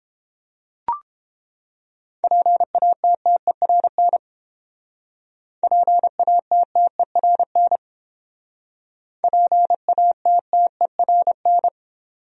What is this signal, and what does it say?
Morse code, heard as a keyed tone, about 30 wpm, 700 Hz. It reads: PATTERN PATTERN PATTERN